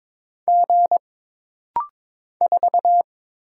Morse code, heard as a keyed tone, about 22 wpm, 700 Hz. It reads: G 4
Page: This rapid-fire timing is geared to push your ICR (Instant Character Recognition) skills.